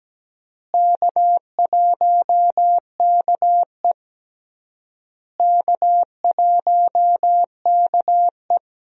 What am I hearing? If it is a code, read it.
Morse code, heard as a keyed tone, about 17 wpm, 700 Hz. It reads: K1KE K1KE
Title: USA Call Signs - 4 Characters - Repeated 3x - 17wpm